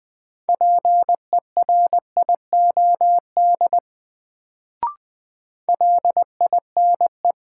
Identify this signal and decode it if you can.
Morse code, heard as a keyed tone, about 20 wpm, 700 Hz. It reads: PERIOD LINE
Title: Sets of 5 Words - New Word Drill - 20wpm